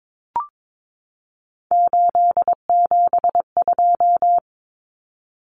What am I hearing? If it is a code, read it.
Morse code, heard as a keyed tone, about 22 wpm, 700 Hz. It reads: 872
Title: Common Cut Numbers - 3-Digit 22wpm